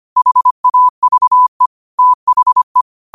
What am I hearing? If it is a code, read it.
Morse code, heard as a keyed tone, about 25 wpm, 1000 Hz. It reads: HAVE THE